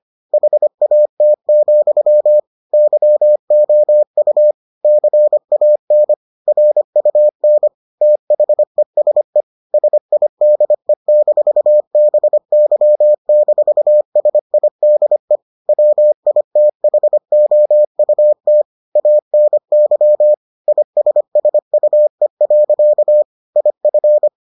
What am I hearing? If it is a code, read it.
Morse code, heard as a keyed tone, about 25 wpm, 600 Hz. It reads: HAT, YOU CAN RUN THESE SIDE-BY-SIDE WITHOUT ANY ISSUE. IF